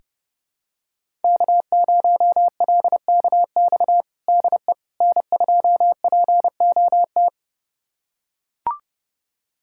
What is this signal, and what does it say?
Morse code, heard as a keyed tone, about 30 wpm, 700 Hz. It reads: K0LKX DE N2POT
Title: USA Callsign DE Callsign - 5 Characters - 30wpm